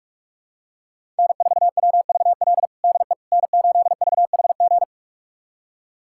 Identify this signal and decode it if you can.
Morse code, heard as a keyed tone, about 46 wpm, 700 Hz. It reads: N4WVL DE N8VHG
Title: USA Callsign DE Callsign - 5 Characters - 45wpm